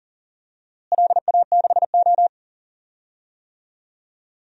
Morse code, heard as a keyed tone, about 40 wpm, 700 Hz. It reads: LA6O
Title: DXCC-100 Call Signs - 4 Characters - Repeated 2x - 40wpm